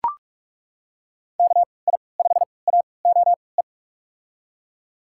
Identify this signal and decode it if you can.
Morse code, heard as a keyed tone, about 41 wpm, spaced as for 25 wpm, 700 Hz. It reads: KI5AOE
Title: USA Call Signs - 6 Characters - 3x Character Spacing - 45wpm